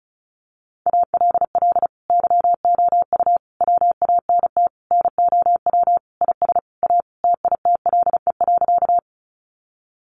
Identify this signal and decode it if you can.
Morse code, heard as a keyed tone, about 34 wpm, 700 Hz. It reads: ALL YOU WANT NOW IS A TITLE.
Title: Sentences from Top 1200 Words 35wpm